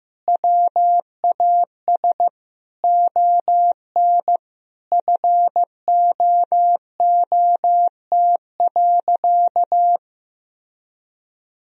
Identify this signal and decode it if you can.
Morse code, heard as a keyed tone, about 15 wpm, 700 Hz. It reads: WAS ON FOOT.
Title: Sentences from Top 1000 Words - Review Part 3 - 15wpm